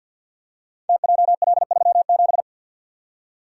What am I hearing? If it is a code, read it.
Morse code, heard as a keyed tone, about 48 wpm, 700 Hz. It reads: TJL37
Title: Code Groups - 5 Letter:Numbers - 50wpm